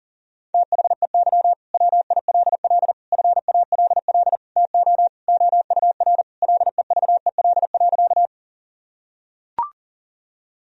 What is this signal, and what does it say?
Morse code, heard as a keyed tone, about 40 wpm, 690 Hz. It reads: THEY WILL FALL TO OUR LEVEL.
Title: Sentences from Top 1000 Words - Review Part 4 - 40wpm